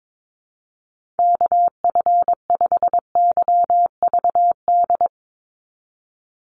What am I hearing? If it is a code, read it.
Morse code, heard as a keyed tone, about 22 wpm, 700 Hz. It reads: KF5YVD
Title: USA Call Signs 22wpm